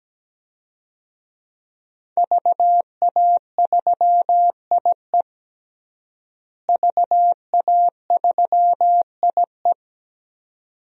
Morse code, heard as a keyed tone, about 17 wpm, 700 Hz. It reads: VA3IE VA3IE
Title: DXCC-100 Call Signs - 5 Characters - Repeated 2x - 17wpm